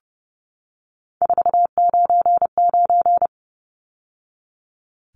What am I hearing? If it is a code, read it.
Morse code, heard as a keyed tone, about 30 wpm, 700 Hz. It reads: 499
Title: Numbers - 3-Digit 30wpm